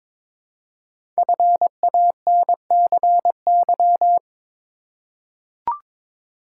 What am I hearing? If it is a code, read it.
Morse code, heard as a keyed tone, about 22 wpm, 700 Hz. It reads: FANCY
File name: 5 Letter Words - First 3 Letters Given - 22wpm